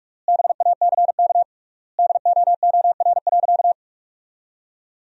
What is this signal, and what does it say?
Morse code, heard as a keyed tone, about 42 wpm, 700 Hz. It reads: BACK DOOR.
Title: 3x Word-Spacing - Sentences from Top 300 Words - 45wpm